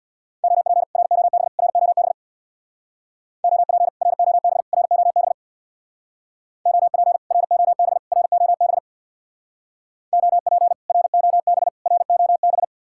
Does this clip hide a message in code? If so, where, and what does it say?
Yes: Morse code from 0.4 to 12.6 s, heard as a keyed tone, about 66 wpm, 700 Hz. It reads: OP ROB ROB OP ROB ROB OP ROB ROB OP ROB ROB